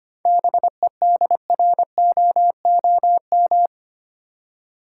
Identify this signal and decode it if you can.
Morse code, heard as a keyed tone, about 25 wpm, 700 Hz. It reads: BEDROOM
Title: Seven Letter Words - 25wpm